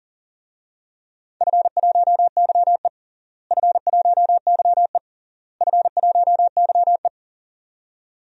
Morse code, heard as a keyed tone, about 37 wpm, 700 Hz. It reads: F1YE F1YE F1YE
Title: DXCC-100 Call Signs - 4 Characters - Repeated 3x - 40wpm